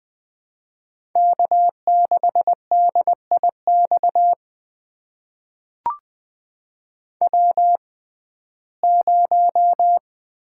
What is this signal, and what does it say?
Morse code, heard as a keyed tone, about 20 wpm, spaced as for 14 wpm, 700 Hz. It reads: K6DIX W0
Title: USA Call Signs - 6x Character Spacing - 20wpm